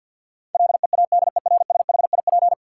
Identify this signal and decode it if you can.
Morse code, heard as a keyed tone, about 47 wpm, 700 Hz. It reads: LEADERSHIP
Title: Sentences from Top 1600 Words - New Word Drill - 50wpm